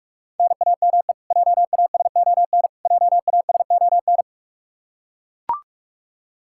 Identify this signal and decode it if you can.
Morse code, heard as a keyed tone, about 42 wpm, 700 Hz. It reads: NAME JASON JASON
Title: K1USN SST QSO Practice 45wpm